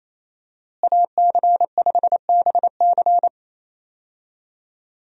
Morse code, heard as a keyed tone, about 28 wpm, 700 Hz. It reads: AC5BC
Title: USA Call Signs - 5 Characters - Repeated 2x - 28wpm